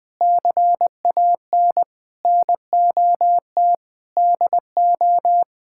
Morse code, heard as a keyed tone, about 19 wpm, 700 Hz. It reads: CAN NOT DO
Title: Sentences from Top 1000 Words - Review Part 6 - 20wpm